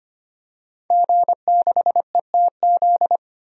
Morse code, heard as a keyed tone, about 25 wpm, 700 Hz. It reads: G6ETZ
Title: DXCC-100 Call Signs - 5 Characters - 25wpm